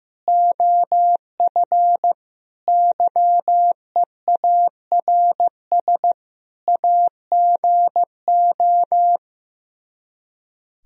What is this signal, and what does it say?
Morse code, heard as a keyed tone, about 15 wpm, 700 Hz. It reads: OF YEARS AGO
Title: Speed-Racing - Sets of 5 Words 15wpm